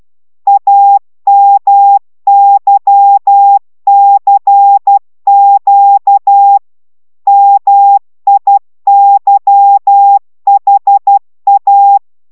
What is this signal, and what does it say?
Morse code, heard as a keyed tone, about 12 wpm, 800 Hz. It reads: AMYCQ MIYHA